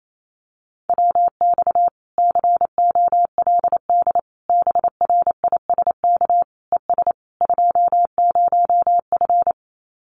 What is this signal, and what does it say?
Morse code, heard as a keyed tone, about 28 wpm, 700 Hz. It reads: WX COLD BRISK ES 20F
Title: Common QSO Phrases Encore 28wpm